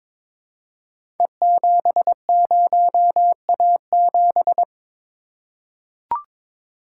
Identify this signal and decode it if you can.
Morse code, heard as a keyed tone, about 22 wpm, 690 Hz. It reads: E70A7
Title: Code Groups - 5 Letter:Numbers - 22wpm